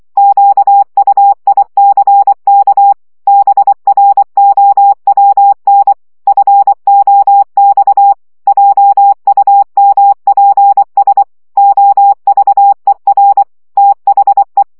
Morse code, heard as a keyed tone, about 24 wpm, 800 Hz. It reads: QUICK BROWN FOX JUMPS OVER THE
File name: CW 20 baud